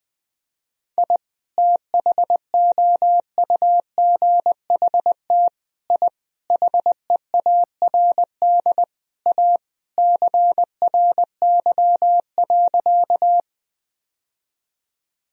Morse code, heard as a keyed tone, about 20 wpm, 700 Hz. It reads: I THOUGHT I HEARD A CRY.
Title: Sentences from Top 1000 Words - Review Part 2 - 20wpm